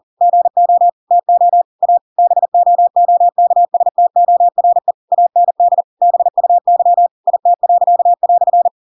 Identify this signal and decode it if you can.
Morse code, heard as a keyed tone, about 40 wpm, 700 Hz. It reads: GO TO A BOOKSTORE AND BUY IT."
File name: Pinocchio / Pinocchio ch8 40 40 ik1hge